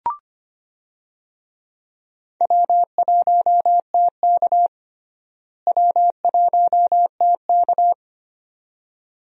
Morse code, heard as a keyed tone, about 25 wpm, 700 Hz. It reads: W1TK W1TK
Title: USA Call Signs - 4 Characters - Repeated 2x - 25wpm